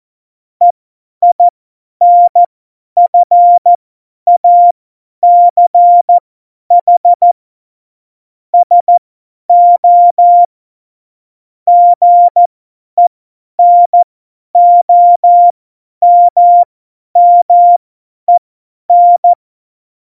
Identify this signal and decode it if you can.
Morse code, heard as a keyed tone, about 14 wpm, spaced as for 10 wpm, 700 Hz. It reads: EINFACH SO GENOMMEN
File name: Morsekurs / Witze / Witz 07 Juli 2020, 50 F70